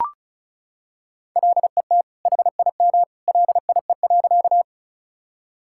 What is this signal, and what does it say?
Morse code, heard as a keyed tone, about 34 wpm, 700 Hz. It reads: LET HIM LIE.
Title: Sentences from Top 1600 Words 35wpm